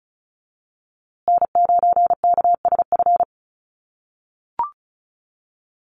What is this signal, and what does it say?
Morse code, heard as a keyed tone, about 34 wpm, 700 Hz. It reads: N9KSF
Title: USA Call Signs Repeated 2x 35wpm